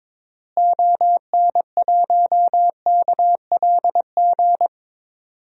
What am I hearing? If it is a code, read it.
Morse code, heard as a keyed tone, about 22 wpm, 700 Hz. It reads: ON1KLG
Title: DXCC-100 Call Signs - 6 Characters - Repeated 3x - 22wpm